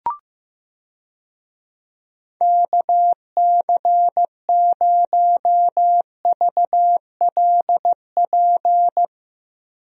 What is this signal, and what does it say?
Morse code, heard as a keyed tone, about 15 wpm, 700 Hz. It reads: KC0VLP
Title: USA Call Signs - 6 Characters - Repeated 2x - 15wpm